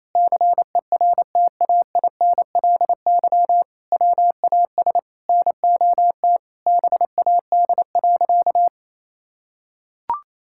Morse code, heard as a keyed tone, about 27 wpm, 700 Hz. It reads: CERTAINLY WAS NOT BAD.
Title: Sentences from Top 1000 Words - Review Part 6 - 28wpm